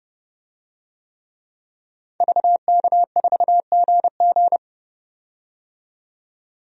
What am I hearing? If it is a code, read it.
Morse code, heard as a keyed tone, about 31 wpm, 700 Hz. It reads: VK4GG
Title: DXCC-100 Call Signs - 5 Characters - 30wpm